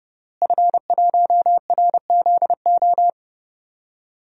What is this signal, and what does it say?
Morse code, heard as a keyed tone, about 30 wpm, 700 Hz. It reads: F1RZO
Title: DXCC-100 Call Signs - 5 Characters - Repeated 2x - 30wpm